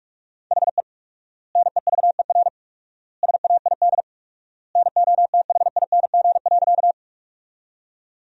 Morse code, heard as a keyed tone, about 46 wpm, 700 Hz. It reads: HE NEVER SAID NOTHING.
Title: 4x Word-Spacing - Sentences from Top 500 Words - 45wpm